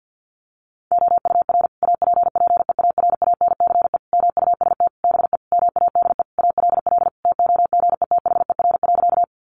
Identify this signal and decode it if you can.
Morse code, heard as a keyed tone, about 50 wpm, 700 Hz. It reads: OUR APPEARANCE MUST BE MADE ALL TOGETHER.
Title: Sentences from Top 1100 Words 50wpm